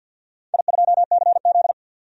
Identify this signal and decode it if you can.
Morse code, heard as a keyed tone, about 52 wpm, 700 Hz. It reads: I2CZ
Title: DXCC-100 Call Signs - 4 Characters - 50wpm